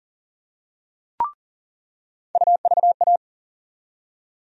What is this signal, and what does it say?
Morse code, heard as a keyed tone, about 40 wpm, 700 Hz. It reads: UVA
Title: USA Call Sign Postfixes - 3 Letters 40wpm